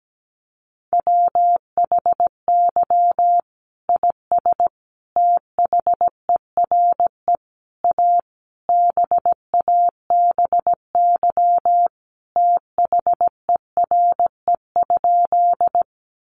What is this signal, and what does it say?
Morse code, heard as a keyed tone, about 17 wpm, 700 Hz. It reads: WHY IS THERE A BABY THERE?
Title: Sentences from Top 1300 Words 17wpm